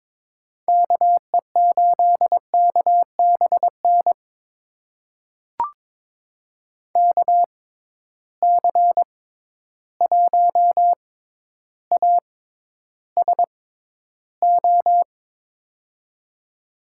Morse code, heard as a keyed tone, about 22 wpm, spaced as for 11 wpm, 700 Hz. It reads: KE8KBN KC1ASO
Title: USA Call Signs - 6x Character Spacing - 22wpm